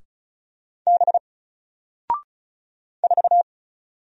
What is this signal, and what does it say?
Morse code, heard as a keyed tone, about 35 wpm, 700 Hz. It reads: B 4
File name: Single-Letter-Number-35wpm